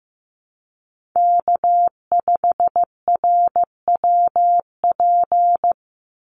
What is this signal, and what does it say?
Morse code, heard as a keyed tone, about 15 wpm, 700 Hz. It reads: K5RWP
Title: NA SOTA Chaser Callsigns 15wpm